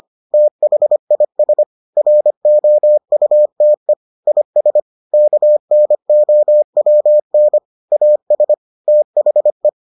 Morse code, heard as a keyed tone, about 25 wpm, 600 Hz. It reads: THIS ROUTE IS KNOWN AS THE